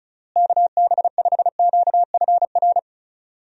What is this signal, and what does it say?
Morse code, heard as a keyed tone, about 36 wpm, 700 Hz. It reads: KB5QFR
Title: USA Call Signs - 6 Characters - Repeated 3x - 35wpm